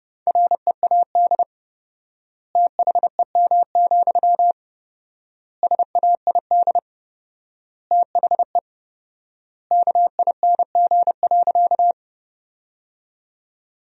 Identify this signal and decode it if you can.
Morse code, heard as a keyed tone, about 30 wpm, 700 Hz. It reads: READ THEM, SAID THE KING.
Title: 4x Word-Spacing - Sentences from Top 400 Words - 30wpm